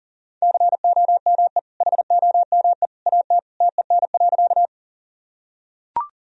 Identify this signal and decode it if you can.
Morse code, heard as a keyed tone, about 40 wpm, 700 Hz. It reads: COME HOME AT TEN.
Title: Sentences from Top 1000 Words - Review Part 2 - 40wpm